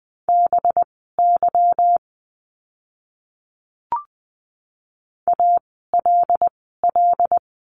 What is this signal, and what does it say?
Morse code, heard as a keyed tone, about 20 wpm, spaced as for 14 wpm, 700 Hz. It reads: BY ALL
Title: 237 Lesson 38f - Letter q - Words 20wpm